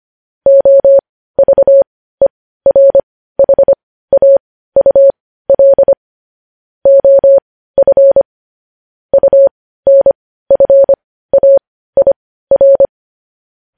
Morse code, heard as a keyed tone, about 25 wpm, spaced as for 15 wpm, 550 Hz. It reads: OVERHAUL OF UNFAIR
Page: News Headlines in Morse Code at 15 WPM